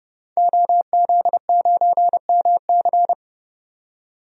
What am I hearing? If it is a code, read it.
Morse code, heard as a keyed tone, about 32 wpm, 700 Hz. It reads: OZ9MC
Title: DXCC-100 Call Signs - 5 Characters - 30wpm